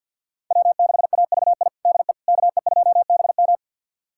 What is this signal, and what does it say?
Morse code, heard as a keyed tone, about 47 wpm, 700 Hz. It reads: W6AVI DE KE2BM
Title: USA Callsign DE Callsign - 5 Characters - 50wpm